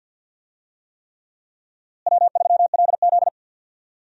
Morse code, heard as a keyed tone, about 48 wpm, 700 Hz. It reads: W3LZ
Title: USA Call Signs - 4 Characters - Repeated 3x - 50wpm